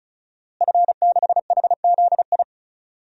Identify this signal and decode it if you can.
Morse code, heard as a keyed tone, about 36 wpm, 700 Hz. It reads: F6HZI
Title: DXCC-100 Call Signs - 5 Characters - Repeated 3x - 35wpm